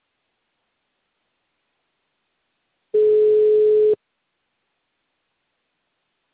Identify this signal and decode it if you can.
Ringback tone (Europe, 425 Hz).